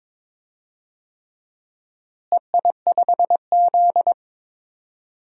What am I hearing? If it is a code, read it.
Morse code, heard as a keyed tone, about 21 wpm, 700 Hz. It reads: EI5Z
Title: DXCC-100 Call Signs - 4 Characters - Repeated 3x - 22wpm